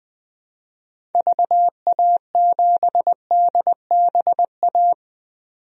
Morse code, heard as a keyed tone, about 20 wpm, 700 Hz. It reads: VA7DBA
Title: DXCC-100 Call Signs - 6 Characters - 20wpm